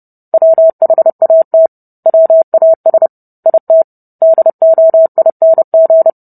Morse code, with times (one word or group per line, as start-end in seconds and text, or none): 0.3-1.7 WHAT
2.1-3.1 WAS
3.5-3.8 IT
4.2-6.1 DOING